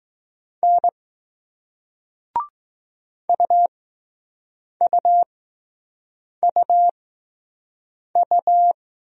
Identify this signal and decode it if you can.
Morse code, heard as a keyed tone, about 20 wpm, spaced as for 6 wpm, 700 Hz. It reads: N UUUU